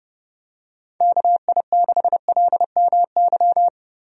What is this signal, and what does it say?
Morse code, heard as a keyed tone, about 32 wpm, 700 Hz. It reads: KI6LMY
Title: USA Call Signs - 6 Characters - Repeated 3x - 30wpm